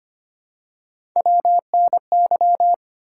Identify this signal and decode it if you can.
Morse code, heard as a keyed tone, about 25 wpm, 700 Hz. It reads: WNY
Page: morse-code-ninja-practice-25wpm